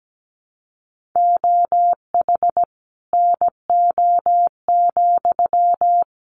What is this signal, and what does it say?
Morse code, heard as a keyed tone, about 17 wpm, 700 Hz. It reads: OH NO,
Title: Sentences from Top 900 Words 17wpm